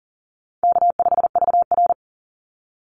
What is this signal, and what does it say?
Morse code, heard as a keyed tone, about 40 wpm, 700 Hz. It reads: K5VR
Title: Easy Call Signs - 40wpm